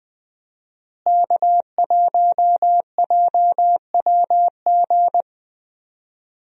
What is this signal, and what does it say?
Morse code, heard as a keyed tone, about 20 wpm, 700 Hz. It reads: K1JWG
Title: USA Call Signs - 5 Characters - Repeated 3x - 20wpm